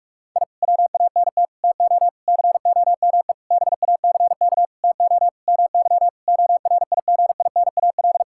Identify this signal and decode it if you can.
Morse code, heard as a keyed tone, about 41 wpm, 700 Hz. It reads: I WANT TO COME BACK TO MY ORIGINAL
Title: Sentences from Top 1000 Words - Review Part 6 - 45wpm